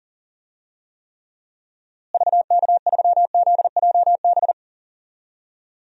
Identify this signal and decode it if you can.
Morse code, heard as a keyed tone, about 40 wpm, 700 Hz. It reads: VK3ZJB
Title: DXCC-100 Call Signs - 6 Characters - Repeated 2x - 40wpm